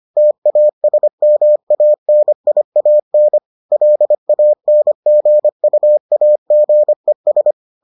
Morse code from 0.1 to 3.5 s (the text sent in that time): TASMANIAN